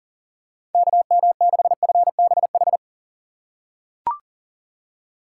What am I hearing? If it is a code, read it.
Morse code, heard as a keyed tone, about 40 wpm, 700 Hz. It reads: KM6FBH
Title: USA Call Signs - 6 Characters - 3x Character Spacing - 40wpm